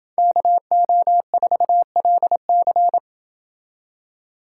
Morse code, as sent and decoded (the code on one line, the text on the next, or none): -.- --- ....- .-.. -.-.
KO4LC